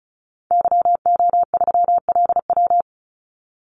Morse code, heard as a keyed tone, about 34 wpm, 700 Hz. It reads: YO3LW
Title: Call Signs Repeated 2x - 35wpm